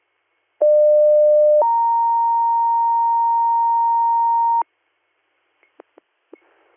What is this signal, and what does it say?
Two-tone page, heard as a one-second tone, then a three-second tone, about 600 then 930 Hz. It is heard from 0.6 to 4.6 s.